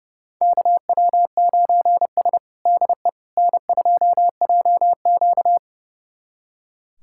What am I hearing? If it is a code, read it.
Morse code, heard as a keyed tone, about 30 wpm, 700 Hz. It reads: KW9S DE N2JQ
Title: USA Callsign DE Callsign - 4 Characters - 30wpm